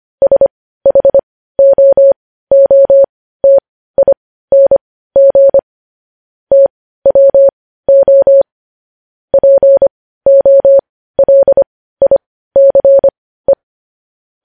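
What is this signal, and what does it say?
Morse code, heard as a keyed tone, about 25 wpm, spaced as for 15 wpm, 550 Hz. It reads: SHOOTING TWO POLICE
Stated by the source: Morse code practice of news headlines at 15 wpm.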